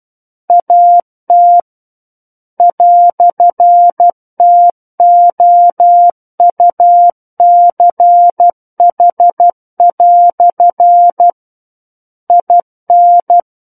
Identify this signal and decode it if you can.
Morse code, heard as a keyed tone, about 12 wpm, 700 Hz. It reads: AT "TOUCH" IN
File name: Book One / 03-Chapter 3 - 12 WPM